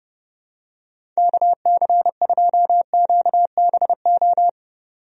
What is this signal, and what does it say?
Morse code, heard as a keyed tone, about 32 wpm, 700 Hz. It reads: KC2QBO
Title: USA Call Signs - 6 Characters - 30wpm